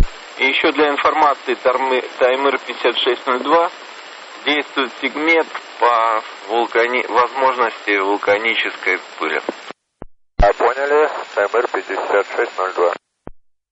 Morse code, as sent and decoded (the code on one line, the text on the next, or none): none